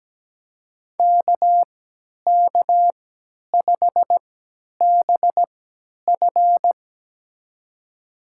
Morse code, heard as a keyed tone, about 17 wpm, spaced as for 10 wpm, 700 Hz. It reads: KK5BF
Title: USA Call Signs - 5 Characters - 3x Character Spacing - 17wpm